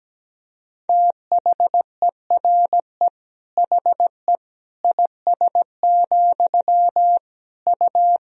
Morse code, heard as a keyed tone, about 17 wpm, 700 Hz. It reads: THERE HE IS, U
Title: Sentences from Top 100 Words and Abbreviations 17wpm